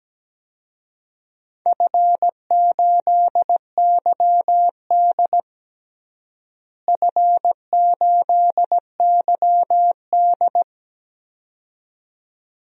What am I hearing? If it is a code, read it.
Morse code, heard as a keyed tone, about 17 wpm, 700 Hz. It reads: F8YD F8YD
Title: DXCC-100 Call Signs - 4 Characters - Repeated 2x - 17wpm